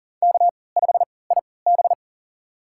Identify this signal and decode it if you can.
Morse code, heard as a keyed tone, about 40 wpm, spaced as for 23 wpm, 700 Hz. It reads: K5IB
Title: USA Call Signs - 4 Characters - 3x Character Spacing - 40wpm